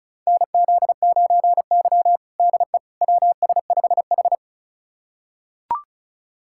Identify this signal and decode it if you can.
Morse code, heard as a keyed tone, about 36 wpm, 700 Hz. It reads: NZ9Y DE WS5H
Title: USA Callsign DE Callsign - 4 Characters - 35wpm